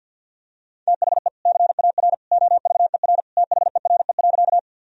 Morse code, heard as a keyed tone, about 53 wpm, 700 Hz. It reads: THE CAR OVER THERE.
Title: Sentences from Top 1000 Words - Review Part 1 - 50wpm